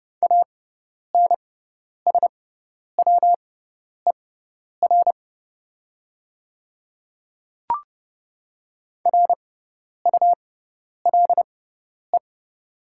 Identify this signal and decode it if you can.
Morse code, heard as a keyed tone, about 30 wpm, spaced as for 10 wpm, 700 Hz. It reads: ANSWER RULE
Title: 6x Character Spacing - Top 500 Words - 30wpm